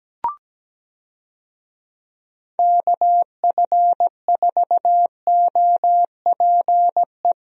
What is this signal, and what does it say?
Morse code, heard as a keyed tone, about 17 wpm, 700 Hz. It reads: KF4OPE
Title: USA Call Signs - 6 Characters - Repeated 2x - 17wpm